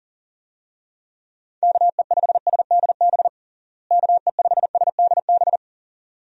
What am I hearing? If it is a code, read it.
Morse code, heard as a keyed tone, about 40 wpm, 700 Hz. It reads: KE5SDB KE5SDB
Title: USA Call Signs - 6 Characters - Repeated 2x - 40wpm